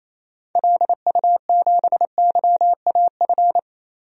Morse code, heard as a keyed tone, about 28 wpm, 700 Hz. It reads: LU7YAF
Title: DXCC-100 Call Signs - 6 Characters - Repeated 3x - 28wpm